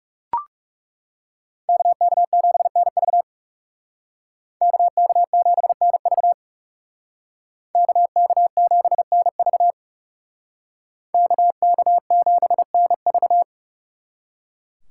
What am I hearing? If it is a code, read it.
Morse code, heard as a keyed tone, about 37 wpm, 700 Hz. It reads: KK7NV KK7NV KK7NV KK7NV
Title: Speed Racing - USA Call Signs 30wpm